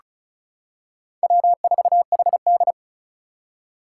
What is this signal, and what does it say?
Morse code, heard as a keyed tone, about 36 wpm, 700 Hz. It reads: W4HD
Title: Four Challenging Letter-Number Combinations 35wpm